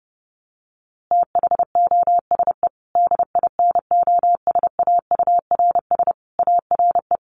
Morse code, heard as a keyed tone, about 30 wpm, 700 Hz. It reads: THOSE DINOSAURS ARE
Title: A Sound Of Thunder 30wpm